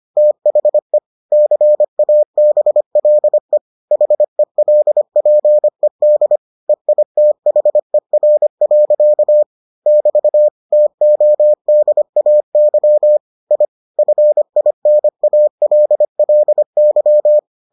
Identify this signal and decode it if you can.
Morse code, heard as a keyed tone, about 25 wpm, 600 Hz. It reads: THE CABLE HELPED EITHER. <BT> TODAY I FINALLY